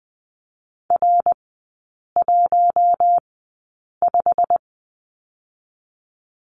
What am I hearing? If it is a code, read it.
Morse code, heard as a keyed tone, about 20 wpm, spaced as for 8 wpm, 700 Hz. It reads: R15
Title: 121 Lesson 20e - Number 3 - Three Characters 20wpm